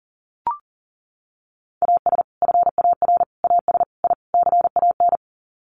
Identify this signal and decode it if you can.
Morse code, heard as a keyed tone, about 40 wpm, 700 Hz. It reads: AS FAR AS I CAN
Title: Sets of 5 Words Encore 40wpm